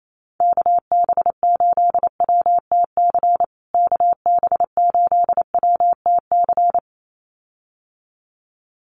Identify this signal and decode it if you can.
Morse code, heard as a keyed tone, about 28 wpm, 700 Hz. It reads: KB8WTC KB8WTC
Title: USA Call Signs Repeated 2x 28wpm